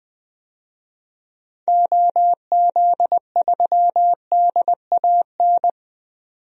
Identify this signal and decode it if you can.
Morse code, heard as a keyed tone, about 20 wpm, 700 Hz. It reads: OZ3DAN
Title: DXCC-100 Call Signs - 6 Characters - 20wpm